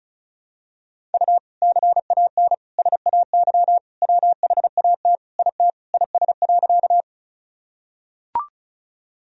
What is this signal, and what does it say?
Morse code, heard as a keyed tone, about 34 wpm, 700 Hz. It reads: U CAN SAY WHAT IT IS.
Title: Sentences from Top 200 Words and Abbreviations 35wpm